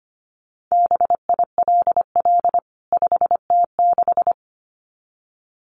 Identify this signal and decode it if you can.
Morse code, heard as a keyed tone, about 25 wpm, 700 Hz. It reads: BILL 5T6
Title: CWT Exchange 25wpm